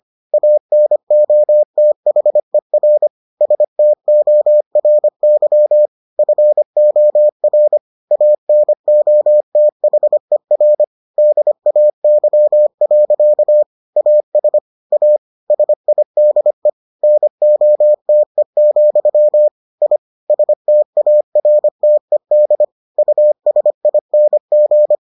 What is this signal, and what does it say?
Morse code, heard as a keyed tone, about 25 wpm, 600 Hz. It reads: ANOTHER STORY FOR ANOTHER DAY. AS A SIDE NOTE, I STARTED USING